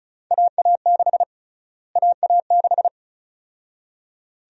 Morse code, heard as a keyed tone, about 34 wpm, 700 Hz. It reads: AA6 AA6
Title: USA Call Sign Prefixes - Repeated 2x - 35wpm